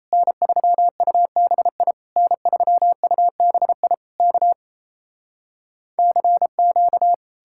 Morse code, heard as a keyed tone, about 37 wpm, 700 Hz. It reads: N3UBI N3UBI K CQ